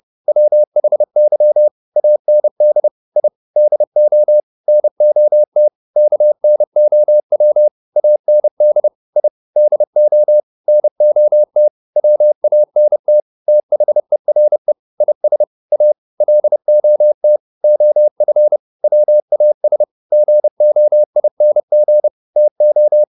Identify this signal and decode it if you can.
Morse code, heard as a keyed tone, about 30 wpm, 600 Hz. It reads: WHY AND I DO NOT KNOW AND I DO NOT WANT THERE IS A LOT OF WAS GOING TO